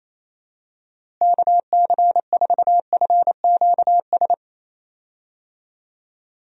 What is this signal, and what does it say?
Morse code, heard as a keyed tone, about 28 wpm, 700 Hz. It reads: KC4FQS
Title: USA Call Signs - 6 Characters - Repeated 3x - 28wpm